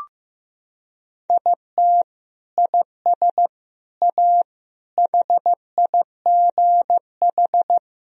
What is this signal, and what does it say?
Morse code, heard as a keyed tone, about 15 wpm, 700 Hz. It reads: IT IS A HIGH
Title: Sentences from Top 1000 Words - Review Part 1 - 15wpm